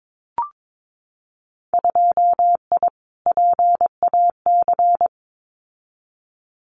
Morse code, heard as a keyed tone, about 22 wpm, 700 Hz. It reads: 2I PAC
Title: Winter Field Day Exchanges Encore 22wpm